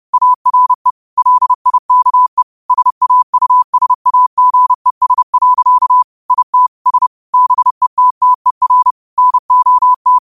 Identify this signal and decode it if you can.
Morse code, heard as a keyed tone, about 30 wpm, 1000 Hz. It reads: ARE LIKE SAUSAGES. IT S BETTER NOT
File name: qotdfiles / 30wpm